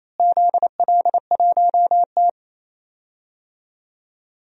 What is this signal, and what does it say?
Morse code, heard as a keyed tone, about 28 wpm, 700 Hz. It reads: ZL1T
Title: DXCC-100 Call Signs - 4 Characters - Repeated 2x - 28wpm